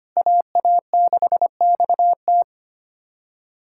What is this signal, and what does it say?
Morse code, heard as a keyed tone, about 25 wpm, 700 Hz. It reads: AA6XT
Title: USA Callsign DE Callsign 25wpm